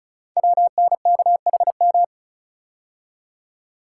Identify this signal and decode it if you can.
Morse code, heard as a keyed tone, about 36 wpm, 700 Hz. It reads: WNKHM
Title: Code Groups - 5 Letters - 35wpm